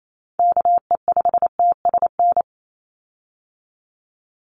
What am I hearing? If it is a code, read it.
Morse code, heard as a keyed tone, about 28 wpm, 700 Hz. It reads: KE5TSN
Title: USA Call Signs 28wpm